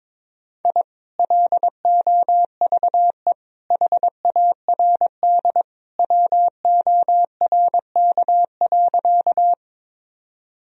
Morse code, heard as a keyed tone, about 22 wpm, 700 Hz. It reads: I LOVE HARD WORK.
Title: Sentences from Top 1000 Words - Review Part 1 - 22wpm